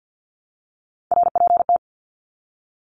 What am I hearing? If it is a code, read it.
Morse code, heard as a keyed tone, about 50 wpm, 700 Hz. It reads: RPT
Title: Abbreviations Repeated 3x - 50wpm